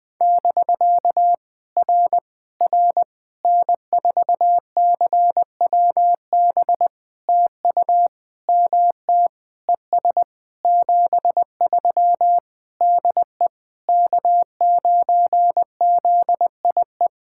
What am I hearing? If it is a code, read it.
Morse code, heard as a keyed tone, about 20 wpm, 690 Hz. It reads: <BK> R R N4CWB TU MT ES 73 DE K9ZIE
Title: POTA QSO Practice - 20wpm